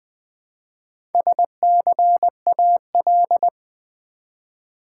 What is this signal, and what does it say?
Morse code, heard as a keyed tone, about 20 wpm, 700 Hz. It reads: SCAL
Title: Build A Word - 1 Letter At A Time - 5-Letter Words - 20wpm